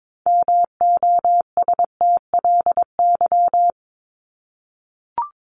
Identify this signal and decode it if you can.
Morse code, heard as a keyed tone, about 22 wpm, 700 Hz. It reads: MOSTLY
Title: Top 1001-1250 Words 22wpm